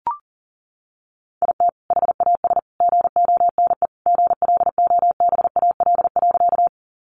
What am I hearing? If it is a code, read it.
Morse code, heard as a keyed tone, about 40 wpm, 700 Hz. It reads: IT HAS GONE GLOBAL.
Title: Sentences from Top 1200 Words 40wpm